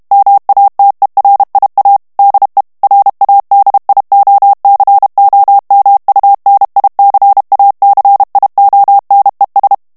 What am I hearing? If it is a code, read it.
Morse code, heard as a keyed tone, about 32 wpm, 800 Hz. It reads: MATERIA DE RADIOCOMUNICACIONES
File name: ESPAGNOL TEXTE EN CLAIR 2 VITESSE 30 MOTS MINUTES